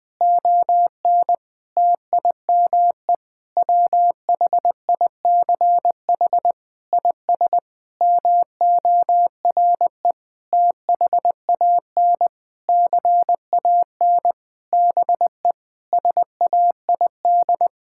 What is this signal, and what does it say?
Morse code, heard as a keyed tone, about 21 wpm, 700 Hz. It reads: ON TIME WHICH IS MORE THAN CAN BE SAID
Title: Sentences with 8-Word Phrases - Spoken Beginning - 20wpm